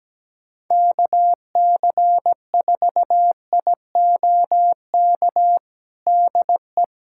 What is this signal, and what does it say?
Morse code, heard as a keyed tone, about 17 wpm, 700 Hz. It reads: KC4IOK DE
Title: USA Callsign DE Callsign - 6 Characters - 17wpm